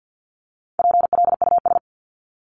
Morse code, heard as a keyed tone, about 50 wpm, 700 Hz. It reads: PLUS
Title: Top 1001-1250 Words 50wpm